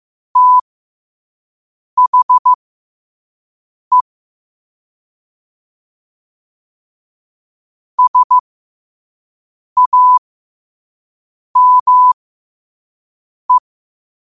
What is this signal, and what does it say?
Morse code, heard as a keyed tone, about 15 wpm, spaced as for 5 wpm, 1000 Hz. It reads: THE SAME
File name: qotdfiles / 5wpm